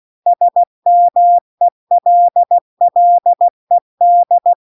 Morse code, heard as a keyed tone, about 16 wpm, 700 Hz. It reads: SMELLED